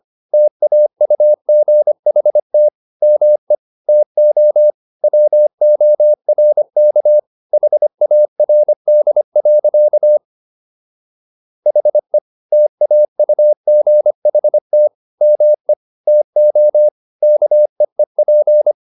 Morse code, heard as a keyed tone, about 25 wpm, 600 Hz. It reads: TAUGHT ME TO WORK HARD. HE TAUGHT ME TO KEEP